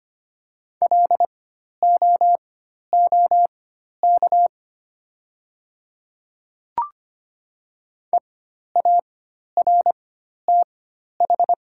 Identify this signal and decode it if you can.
Morse code, heard as a keyed tone, about 25 wpm, spaced as for 12 wpm, 690 Hz. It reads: LOOK EARTH
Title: 4x Character Spacing - Top 500 Words - 25wpm